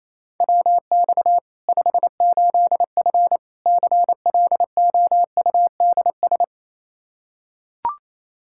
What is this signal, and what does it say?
Morse code, heard as a keyed tone, about 28 wpm, 700 Hz. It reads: WX 58F CLOUDS
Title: Practice QSO - Exchange 2 - Info - 28wpm